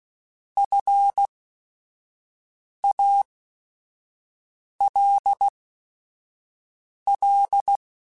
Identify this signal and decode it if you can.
Morse code, heard as a keyed tone, about 16 wpm, spaced as for 5 wpm, 800 Hz. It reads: FALL
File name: Quotes / Quotes-mp3s / ChildAsleep @5